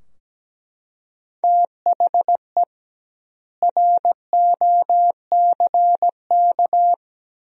Morse code, heard as a keyed tone, about 17 wpm, 700 Hz. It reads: THE ROCK
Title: 2x Word-Spacing - Sentences from Top 400 Words - 17wpm